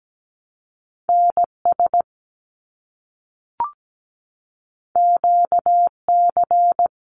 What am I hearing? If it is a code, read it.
Morse code, heard as a keyed tone, about 17 wpm, 700 Hz. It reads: NS QC
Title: Canadian Provinces and Territories - Repeated 3x - 17wpm